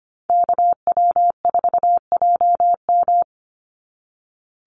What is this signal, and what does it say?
Morse code, heard as a keyed tone, about 25 wpm, 700 Hz. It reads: KW4JM
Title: NA SOTA Activator Callsigns 25wpm